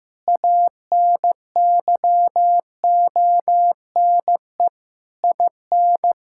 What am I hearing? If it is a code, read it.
Morse code, heard as a keyed tone, about 15 wpm, 700 Hz. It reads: ANYONE IN
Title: Sentences from Top 1000 Words - Review Part 5 - 15wpm